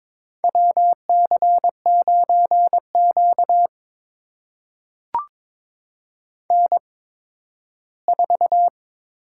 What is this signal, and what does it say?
Morse code, heard as a keyed tone, about 22 wpm, spaced as for 13 wpm, 700 Hz. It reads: WC9Q N4
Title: USA Call Signs - 8x Character Spacing - 22wpm